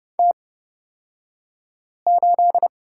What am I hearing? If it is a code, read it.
Morse code, heard as a keyed tone, about 30 wpm, spaced as for 5 wpm, 700 Hz. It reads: T8
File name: Lesson 37d - Letter x - Two Characters 30wpm